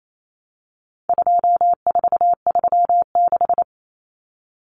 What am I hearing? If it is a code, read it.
Morse code, heard as a keyed tone, about 28 wpm, 700 Hz. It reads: 2436
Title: Numbers - 4-Digit 28wpm